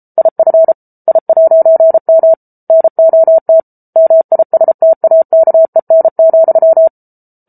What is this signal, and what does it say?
Morse code, heard as a keyed tone, about 34 wpm, 650 Hz. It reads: IF I'M NOT MISTAKEN,